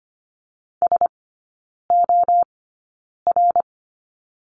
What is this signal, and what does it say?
Morse code, heard as a keyed tone, about 25 wpm, spaced as for 9 wpm, 700 Hz. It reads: SOR
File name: course / 25:10wpm / 100 Lesson 17e - Letter u - Three Characters 25wpm